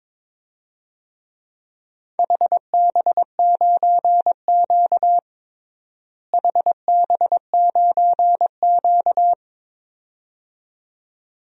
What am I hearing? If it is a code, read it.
Morse code, heard as a keyed tone, about 22 wpm, 700 Hz. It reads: HB9Q HB9Q
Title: DXCC-100 Call Signs - 4 Characters - Repeated 2x - 22wpm